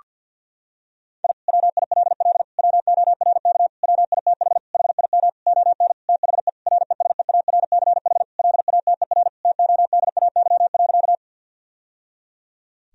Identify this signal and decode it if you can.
Morse code, heard as a keyed tone, about 50 wpm, 700 Hz. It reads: I WILL WORK WITH HIM ON THE RESEARCH LATER TODAY.
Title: Sentences from Top 1000 Words - Review Part 4 - 50wpm